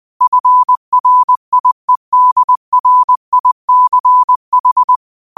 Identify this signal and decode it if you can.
Morse code, heard as a keyed tone, about 20 wpm, 1000 Hz. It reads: FRIEDRICH
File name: qotdfiles / 20wpm